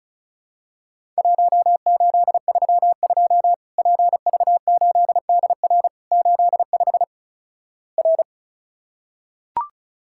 Morse code, heard as a keyed tone, about 36 wpm, 700 Hz. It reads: 1832 PV8DR 85 R
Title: Worked All Europe DX Contest - QTC Exchange - 35wpm